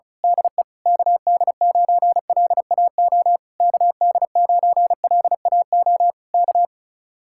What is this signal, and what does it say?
Morse code, heard as a keyed tone, about 32 wpm, 700 Hz. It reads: DE KD9LAO KD9LAO K
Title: Speed-Racing Common QSO Phrases Encore - 35wpm